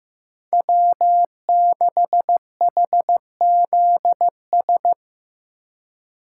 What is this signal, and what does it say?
Morse code, heard as a keyed tone, about 15 wpm, 700 Hz. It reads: W6HZS
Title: USA Call Signs - 2x Character Spacing - 15wpm